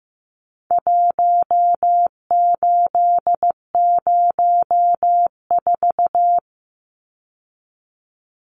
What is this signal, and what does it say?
Morse code, heard as a keyed tone, about 15 wpm, 700 Hz. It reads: 1804
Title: Numbers - Contest Style - 4-Digit 15wpm